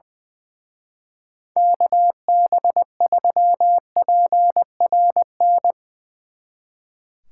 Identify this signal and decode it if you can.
Morse code, heard as a keyed tone, about 21 wpm, 700 Hz. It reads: KB3PRN
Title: Speed Racing - USA 2x3 Call Signs 20wpm